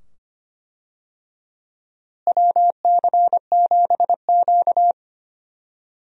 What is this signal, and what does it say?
Morse code, heard as a keyed tone, about 25 wpm, 700 Hz. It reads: WC7Q
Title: USA Call Signs - 4 Characters - Repeated 2x - 25wpm